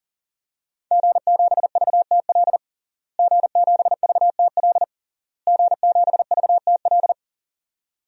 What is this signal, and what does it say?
Morse code, heard as a keyed tone, about 40 wpm, 700 Hz. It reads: G7VTL G7VTL G7VTL
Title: DXCC-100 Call Signs - 5 Characters - Repeated 3x - 40wpm